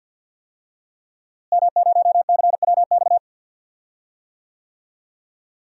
Morse code, heard as a keyed tone, about 47 wpm, 700 Hz. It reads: M0CWX
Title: DXCC-100 Call Signs - 5 Characters - Repeated 3x - 50wpm